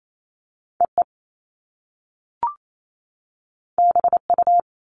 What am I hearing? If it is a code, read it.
Morse code, heard as a keyed tone, about 28 wpm, 700 Hz. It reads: EE BU
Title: Top Two Letter Combinations in Words 28wpm